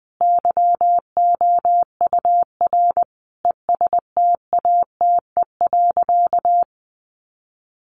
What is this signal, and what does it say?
Morse code, heard as a keyed tone, about 20 wpm, 700 Hz. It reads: YOUR ESTATE.
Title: Sentences from Top 1200 Words 20wpm